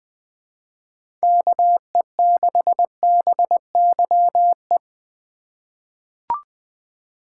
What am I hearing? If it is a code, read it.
Morse code, heard as a keyed tone, about 20 wpm, 700 Hz. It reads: KE6BYE
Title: USA Call Signs - 6 Characters - 4x Character Spacing - 20wpm